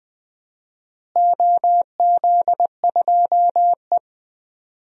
Morse code, heard as a keyed tone, about 20 wpm, 700 Hz. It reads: OZ2E
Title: DXCC-100 Call Signs - 4 Characters - Repeated 2x - 20wpm